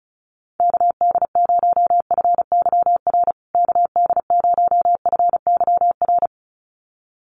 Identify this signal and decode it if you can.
Morse code, heard as a keyed tone, about 34 wpm, 700 Hz. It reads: KD0FYR KD0FYR
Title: USA Call Signs Repeated 2x 35wpm